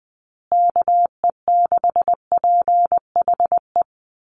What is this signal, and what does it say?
Morse code, heard as a keyed tone, about 20 wpm, 700 Hz. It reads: KE6PHE
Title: USA Call Signs Repeated 3x 20wpm